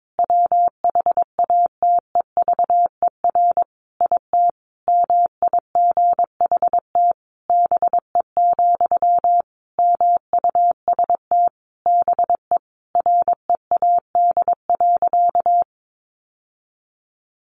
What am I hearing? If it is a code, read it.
Morse code, heard as a keyed tone, about 22 wpm, 700 Hz. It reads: WHATEVER IT MIGHT BE, MUST BE READ.
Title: Sentences from Top 1100 Words 22wpm